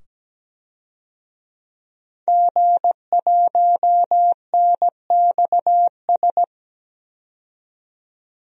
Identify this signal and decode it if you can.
Morse code, heard as a keyed tone, about 17 wpm, 700 Hz. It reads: G1NXS
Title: DXCC-100 Call Signs - 5 Characters - 17wpm